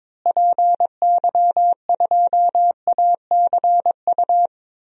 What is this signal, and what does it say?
Morse code, heard as a keyed tone, about 21 wpm, 700 Hz. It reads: PY2ACU
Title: DXCC-100 Call Signs - 6 Characters - Repeated 3x - 22wpm